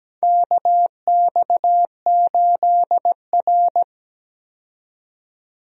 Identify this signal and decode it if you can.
Morse code, heard as a keyed tone, about 17 wpm, 700 Hz. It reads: KX8R
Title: USA Call Signs - 4 Characters - 17wpm